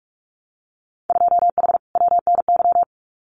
Morse code, heard as a keyed tone, about 46 wpm, 700 Hz. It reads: 2H WNY
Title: Winter Field Day Exchanges Encore 45wpm